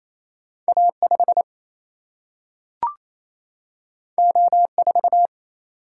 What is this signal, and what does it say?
Morse code, heard as a keyed tone, about 28 wpm, 700 Hz. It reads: A 5 O 4
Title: Single-Letter Single-Number 28wpm